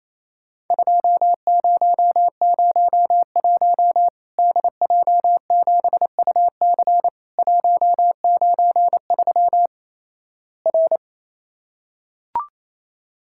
Morse code, heard as a keyed tone, about 28 wpm, 700 Hz. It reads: 2001 DJ7UC 193 R
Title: Worked All Europe DX Contest - QTC Exchange - 28wpm